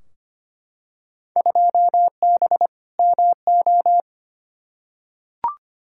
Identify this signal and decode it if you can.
Morse code, heard as a keyed tone, about 25 wpm, 700 Hz. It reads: 2B MO
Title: Field Day Exchanges - 25wpm